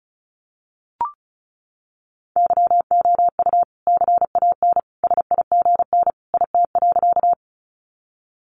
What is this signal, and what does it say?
Morse code, heard as a keyed tone, about 34 wpm, 700 Hz. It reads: YOU CAN SIGN IT.
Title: Sentences from Top 1000 Words 35wpm